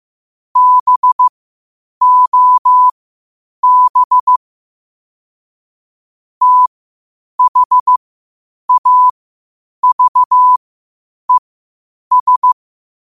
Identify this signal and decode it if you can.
Morse code, heard as a keyed tone, about 15 wpm, spaced as for 8 wpm, 1000 Hz. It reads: BOB THAVES
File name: qotdfiles / 8wpm